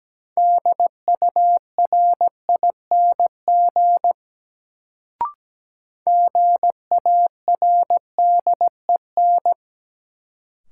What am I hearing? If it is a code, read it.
Morse code, heard as a keyed tone, about 17 wpm, 700 Hz. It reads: DURING GARDEN
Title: Sentences from Top 700 Words - New Word Drill - 17wpm